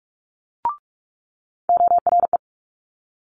Morse code, heard as a keyed tone, about 46 wpm, 700 Hz. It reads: ORE
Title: Multi-letter Phonemes Quick Fire 45wpm